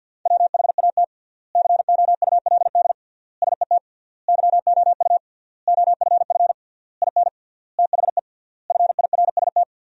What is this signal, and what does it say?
Morse code, heard as a keyed tone, about 50 wpm, 700 Hz. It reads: WHAT COULD SET YOU OFF IN THE FIRST
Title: 3x Word-Spacing - Sentences from Top 200 Words - 50wpm